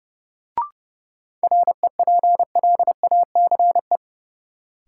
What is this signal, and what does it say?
Morse code, heard as a keyed tone, about 30 wpm, 700 Hz. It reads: REPLACE
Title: Sentences from Top 1400 Words - New Word Drill - 30wpm